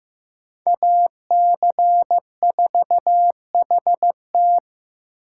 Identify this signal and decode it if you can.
Morse code, heard as a keyed tone, about 18 wpm, 700 Hz. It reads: AC4HT
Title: Speed Racing - USA Call Signs 15wpm